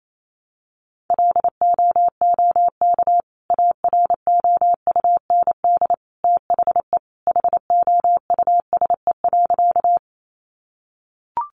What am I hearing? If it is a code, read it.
Morse code, heard as a keyed tone, about 28 wpm, 700 Hz. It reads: LOOK AROUND THE HOUSE.
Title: Sentences from Top 600 Words 28wpm